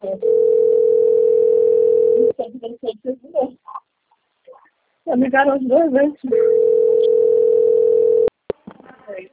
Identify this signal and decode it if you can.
Ringback tone (North America), 440 + 480 Hz, with speech.
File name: rg-1600-unknown- - - .27673